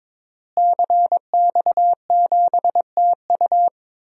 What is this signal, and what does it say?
Morse code, heard as a keyed tone, about 22 wpm, 700 Hz. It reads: CX7TU
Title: DXCC-100 Call Signs - 5 Characters - Repeated 3x - 22wpm